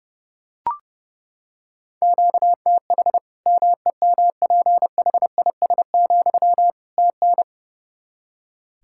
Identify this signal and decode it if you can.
Morse code, heard as a keyed tone, about 30 wpm, 700 Hz. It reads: QTH MEMPHIS, TN
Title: Practice QSO - Exchange 1 - Beginning - 30wpm